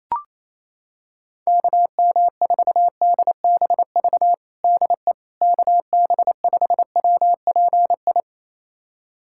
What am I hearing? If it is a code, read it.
Morse code, heard as a keyed tone, about 27 wpm, 700 Hz. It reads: KM4DBV DE KB5WPI
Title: USA Callsign DE Callsign - 6 Characters - 28wpm